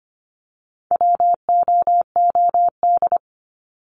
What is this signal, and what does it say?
Morse code, heard as a keyed tone, about 25 wpm, 700 Hz. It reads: WOOD
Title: Top 300 Words 25wpm